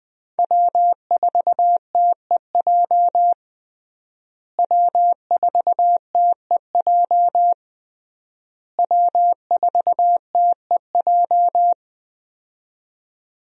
Morse code, heard as a keyed tone, about 20 wpm, 700 Hz. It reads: W4TEJ W4TEJ W4TEJ
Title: USA Call Signs - 5 Characters - Repeated 3x - 20wpm